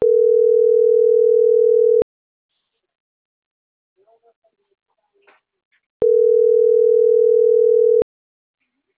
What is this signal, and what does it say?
Ringback tone (North America), 440 + 480 Hz.